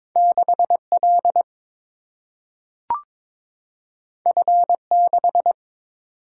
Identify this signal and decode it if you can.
Morse code, heard as a keyed tone, about 22 wpm, 700 Hz. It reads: 6 L F 6
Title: Single-Letter Single-Number 22wpm